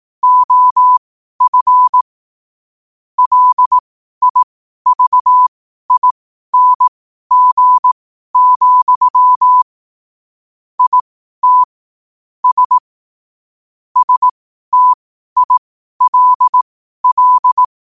Morse code, heard as a keyed tone, about 18 wpm, spaced as for 12 wpm, 1000 Hz. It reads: OF LIVING, IT S STILL
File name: qotdfiles / 12wpm